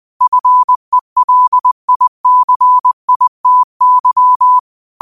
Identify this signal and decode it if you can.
Morse code, heard as a keyed tone, about 20 wpm, 1000 Hz. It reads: FELICITY
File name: qotdfiles / 20wpm